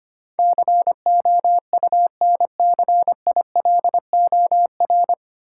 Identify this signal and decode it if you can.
Morse code, heard as a keyed tone, about 25 wpm, 700 Hz. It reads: COUNCILOR
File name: Nine Letter Words - 25wpm